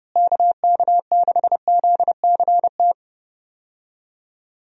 Morse code, heard as a keyed tone, about 30 wpm, 700 Hz. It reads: KK6ZCT
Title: USA Call Signs - 6 Characters - Repeated 3x - 30wpm